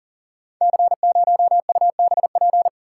Morse code, heard as a keyed tone, about 38 wpm, 690 Hz. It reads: C0UBP
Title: Code Groups - 5 Letter:Numbers - 40wpm